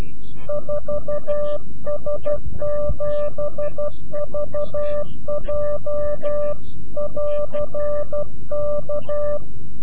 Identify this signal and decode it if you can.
Morse code, heard as a keyed tone, about 12 wpm, 590 Hz. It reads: 4S7VJ <AR> K
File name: CW-QSO / CQ Call / CQ-QSO